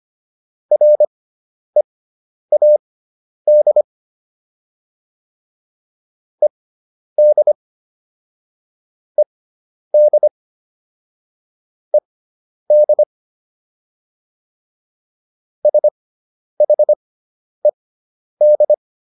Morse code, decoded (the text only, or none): READ ED ED ED SHED